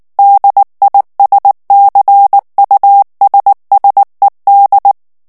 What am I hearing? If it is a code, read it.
Morse code, heard as a keyed tone, about 19 wpm, 800 Hz. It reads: DISCUSSED
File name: ANGLAIS texte clair 3 vitesse 18 mots minute